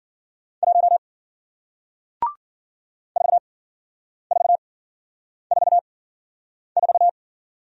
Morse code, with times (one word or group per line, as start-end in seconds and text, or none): 0.6-1.0 1
3.2-7.1 4444